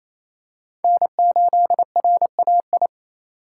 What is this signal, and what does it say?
Morse code, heard as a keyed tone, about 28 wpm, 700 Hz. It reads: N8RAI
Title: USA Call Signs - 5 Characters - 4x Character Spacing - 28wpm